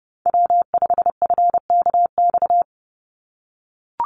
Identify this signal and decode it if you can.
Morse code, heard as a keyed tone, about 30 wpm, 700 Hz. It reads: W5FKX
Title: Call Signs - 30wpm